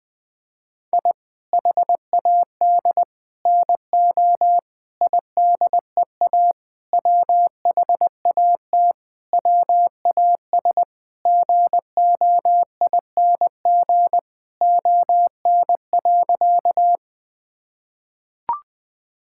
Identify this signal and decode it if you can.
Morse code, heard as a keyed tone, about 20 wpm, 700 Hz. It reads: I HAD NO IDEA WHAT WAS GOING ON.
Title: Sentences with 8-Word Phrases - 20wpm